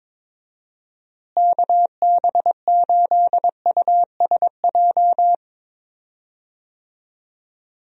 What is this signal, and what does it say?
Morse code, heard as a keyed tone, about 22 wpm, 700 Hz. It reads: KB8USJ
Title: USA Call Signs - 6 Characters - Repeated 3x - 22wpm